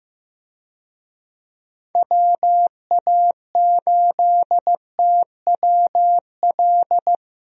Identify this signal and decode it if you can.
Morse code, heard as a keyed tone, about 15 wpm, 700 Hz. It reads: WA8TWL
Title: USA Call Signs - 6 Characters - Repeated 3x - 15wpm